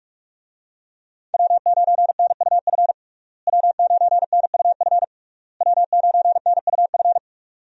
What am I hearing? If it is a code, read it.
Morse code, heard as a keyed tone, about 48 wpm, 700 Hz. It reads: W9NUF W9NUF W9NUF
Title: USA Call Signs - 5 Characters - Repeated 3x - 45wpm